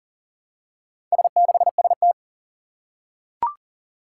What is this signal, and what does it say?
Morse code, heard as a keyed tone, about 40 wpm, 700 Hz. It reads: S6ST
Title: Four Challenging Letter-Number Combinations 40wpm